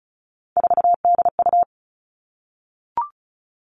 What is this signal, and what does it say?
Morse code, heard as a keyed tone, about 34 wpm, 700 Hz. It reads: 4DU
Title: Three Challenging Letter-Number Combinations 35wpm